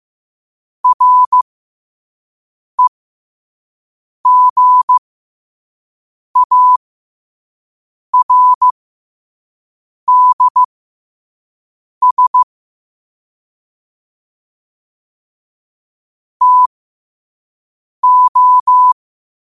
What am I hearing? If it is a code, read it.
Morse code, heard as a keyed tone, about 15 wpm, spaced as for 5 wpm, 1000 Hz. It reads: REGARDS TO
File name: qotdfiles / 5wpm